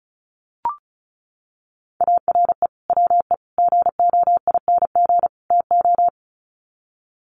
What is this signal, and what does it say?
Morse code, heard as a keyed tone, about 34 wpm, 700 Hz. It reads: ARE WE GOING TO
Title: Sets of 4 Words 35wpm